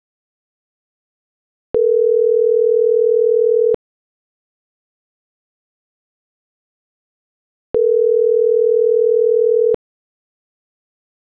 Ringback tone (North America), 440 + 480 Hz.